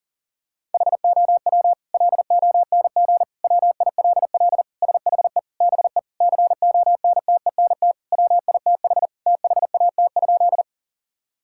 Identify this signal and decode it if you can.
Morse code, heard as a keyed tone, about 37 wpm, 700 Hz. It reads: HOW LONG WILL SHE BE CONTENT WITH THAT?
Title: Sentences from Top 1000 Words - Review Part 6 - 40wpm